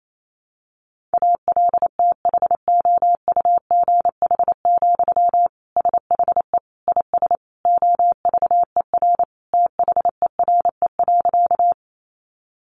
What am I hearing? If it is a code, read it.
Morse code, heard as a keyed tone, about 28 wpm, 700 Hz. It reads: ALTHOUGH, SHE IS OVER THERE.
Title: Sentences from Top 600 Words 28wpm